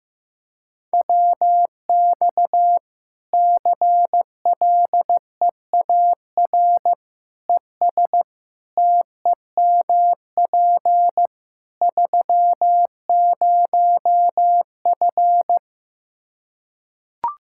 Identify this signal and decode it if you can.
Morse code, heard as a keyed tone, about 15 wpm, 700 Hz. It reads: WX CLEAR ES TEMP 30F
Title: Practice QSO - Exchange 2 - Info - 15wpm